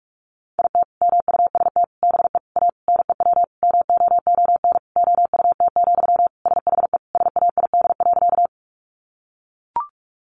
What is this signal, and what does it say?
Morse code, heard as a keyed tone, about 44 wpm, 700 Hz. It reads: IT MUST BE A NEW MOON OUT, SHE SAID.
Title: Sentences from Top 500 Words 45wpm